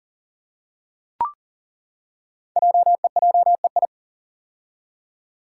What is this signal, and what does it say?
Morse code, heard as a keyed tone, about 39 wpm, 690 Hz. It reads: JEJEI
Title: Code Groups - 5 Letter:Numbers - 40wpm